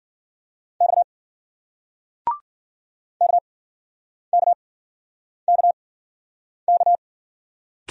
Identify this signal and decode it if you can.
Morse code, heard as a keyed tone, about 56 wpm, spaced as for 9 wpm, 700 Hz. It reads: X KKKK